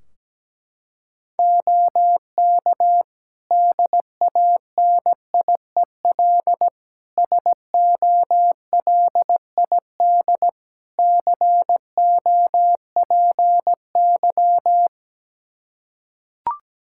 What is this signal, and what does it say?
Morse code, heard as a keyed tone, about 17 wpm, 700 Hz. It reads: OK DANIEL SOLID COPY
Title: K1USN SST QSO Practice 17wpm